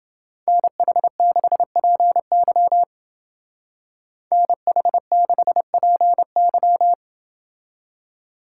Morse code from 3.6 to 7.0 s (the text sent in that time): NH6PY